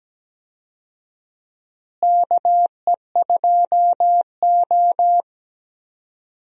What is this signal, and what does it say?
Morse code, heard as a keyed tone, about 17 wpm, 700 Hz. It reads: KE2O
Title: USA Call Signs - 4 Characters - Repeated 2x - 17wpm